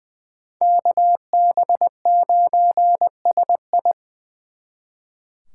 Morse code, heard as a keyed tone, about 24 wpm, 700 Hz. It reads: KB9SI
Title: Speed Racing - USA 2x2 Call Signs 20wpm